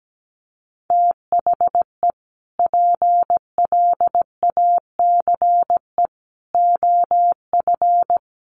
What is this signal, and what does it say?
Morse code, heard as a keyed tone, about 17 wpm, 700 Hz. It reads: THE PLACE OF
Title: Sentences from Top 1500 Words 17wpm